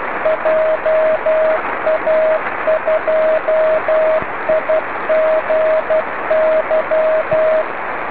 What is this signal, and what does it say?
Morse code, heard as a keyed tone, about 12 wpm, 630 Hz, with radio noise. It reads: JA2IGY